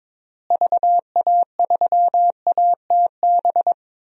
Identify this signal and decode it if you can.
Morse code, heard as a keyed tone, about 21 wpm, 700 Hz. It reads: VA3ATB
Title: DXCC-100 Call Signs - 6 Characters - Repeated 3x - 22wpm